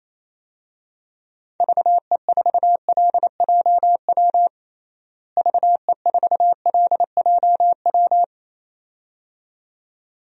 Morse code, heard as a keyed tone, about 28 wpm, 700 Hz. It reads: VE4LJW VE4LJW
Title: DXCC-100 Call Signs - 6 Characters - Repeated 2x - 28wpm